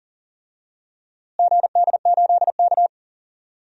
Morse code, heard as a keyed tone, about 40 wpm, 700 Hz. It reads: GD8K
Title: DXCC-100 Call Signs - 4 Characters - Repeated 3x - 40wpm